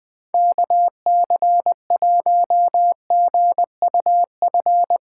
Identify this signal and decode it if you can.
Morse code, heard as a keyed tone, about 21 wpm, 700 Hz. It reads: KC1GUF
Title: SOTA QSO Practice 20wpm